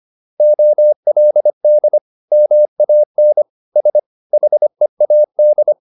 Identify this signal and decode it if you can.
Morse code, heard as a keyed tone, about 25 wpm, 600 Hz. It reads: OLD MAN S HEAD
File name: the old man and the sea part 1 25wpm